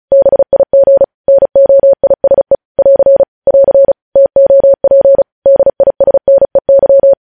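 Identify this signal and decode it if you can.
Morse code, heard as a keyed tone, about 36 wpm, 550 Hz. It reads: BIG NOISE <AR> <AR> TOP DISNEY
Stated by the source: Morse code practice reading news headlines at 35 words per minute.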